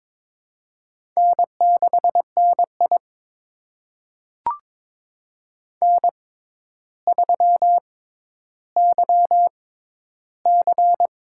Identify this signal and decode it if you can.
Morse code, heard as a keyed tone, about 22 wpm, spaced as for 11 wpm, 700 Hz. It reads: N6NI N3YC
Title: USA Call Signs - 4 Characters - 6x Character Spacing - 22wpm